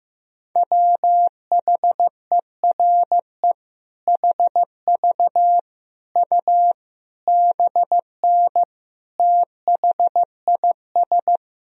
Morse code, heard as a keyed tone, about 16 wpm, 700 Hz. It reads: WHERE HV U BN THIS
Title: Sentences from Top 300 Words and Abbreviations 15wpm